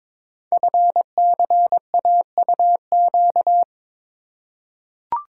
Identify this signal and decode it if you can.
Morse code, heard as a keyed tone, about 22 wpm, 700 Hz. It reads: FCAUQ
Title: Code Groups - 5 Letters - 22wpm